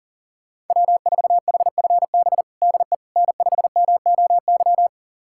Morse code, heard as a keyed tone, about 37 wpm, 700 Hz. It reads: W4HFB DE N5MOY
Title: USA Callsign DE Callsign - 5 Characters - 40wpm